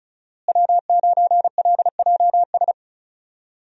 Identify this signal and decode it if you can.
Morse code, heard as a keyed tone, about 34 wpm, 700 Hz. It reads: W9LJS
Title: USA Call Signs - 5 Characters - Repeated 3x - 35wpm